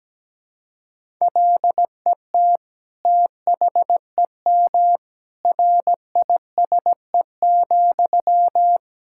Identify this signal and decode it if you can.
Morse code, heard as a keyed tone, about 17 wpm, 700 Hz. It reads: LET THEM RISE,
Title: Sentences from Top 1000 Words - Review Part 6 - 17wpm